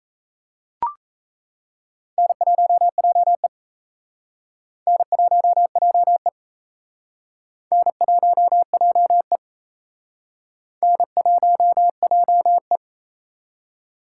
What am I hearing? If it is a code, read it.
Morse code, heard as a keyed tone, about 34 wpm, 700 Hz. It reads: N1JE N1JE N1JE N1JE
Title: Speed Racing - USA 1x2 Call Signs 28wpm